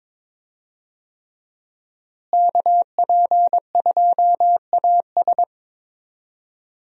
Morse code, heard as a keyed tone, about 21 wpm, 700 Hz. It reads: KP2AS
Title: USA Call Signs - 5 Characters - Repeated 3x - 22wpm